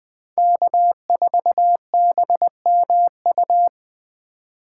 Morse code, heard as a keyed tone, about 24 wpm, 700 Hz. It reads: K4BMU
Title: Speed Racing - USA 1x3 Call Signs 20wpm